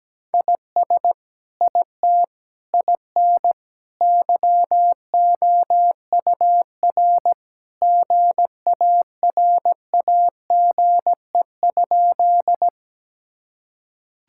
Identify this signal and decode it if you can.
Morse code, heard as a keyed tone, about 17 wpm, 700 Hz. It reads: IS IT IN YOUR GARAGE?
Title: Sentences from Top 1600 Words 17wpm